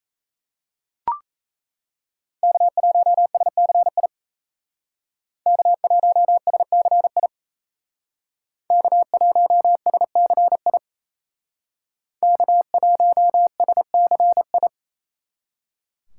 Morse code, heard as a keyed tone, about 35 wpm, 700 Hz. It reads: K1SCI K1SCI K1SCI K1SCI
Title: Speed Racing - USA 1x3 Call Signs 28wpm